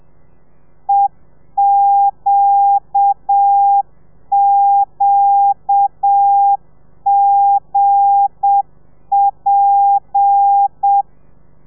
Morse code, heard as a keyed tone, about 7 wpm, 800 Hz. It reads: EQQGP